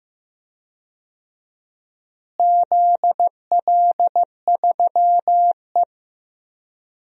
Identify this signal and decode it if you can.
Morse code, heard as a keyed tone, about 16 wpm, 700 Hz. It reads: ZL3E
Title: DXCC-100 Call Signs - 4 Characters - Repeated 3x - 15wpm